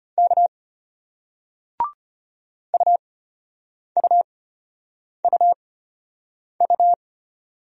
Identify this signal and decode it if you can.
Morse code, heard as a keyed tone, about 34 wpm, spaced as for 8 wpm, 700 Hz. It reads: K UUUU